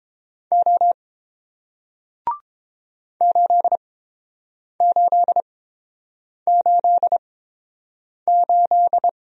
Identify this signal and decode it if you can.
Morse code, heard as a keyed tone, about 31 wpm, spaced as for 7 wpm, 700 Hz. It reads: O 8888